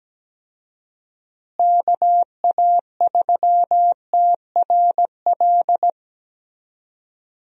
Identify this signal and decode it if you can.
Morse code, heard as a keyed tone, about 17 wpm, 700 Hz. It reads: KA3TRL
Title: USA Call Signs - 6 Characters - Repeated 3x - 17wpm